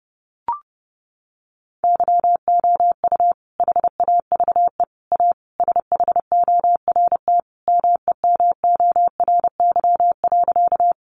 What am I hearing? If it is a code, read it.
Morse code, heard as a keyed tone, about 30 wpm, 700 Hz. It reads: YOU HAVE A SHORT MEMORY.
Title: Sentences from Top 1300 Words 30wpm